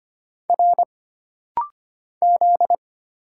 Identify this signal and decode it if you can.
Morse code, heard as a keyed tone, about 25 wpm, 700 Hz. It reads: R Z
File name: Single Character (Letter:Number:Punctuation:Prosign) - Rapid-Fire - 25wpm